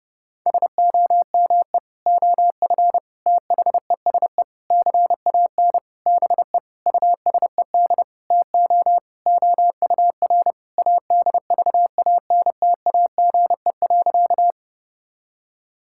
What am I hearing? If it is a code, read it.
Morse code, heard as a keyed tone, about 32 wpm, 700 Hz. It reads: SOME OF THESE CAN BE USED TO OUR ADVANTAGE.
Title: Sentences from Top 1000 Words - Review Part 6 - 30wpm